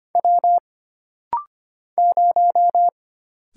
Morse code, heard as a keyed tone, about 25 wpm, 700 Hz. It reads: W 0
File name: Single Character (Letter:Number:Punctuation:Prosign) - Rapid-Fire - 25wpm